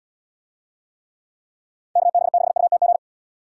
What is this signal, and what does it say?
Morse code, heard as a keyed tone, about 76 wpm, 700 Hz. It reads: KB6UTG